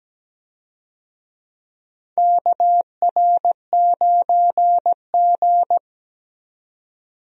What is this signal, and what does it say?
Morse code, heard as a keyed tone, about 17 wpm, 700 Hz. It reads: KR9G
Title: USA Call Signs - 4 Characters - 17wpm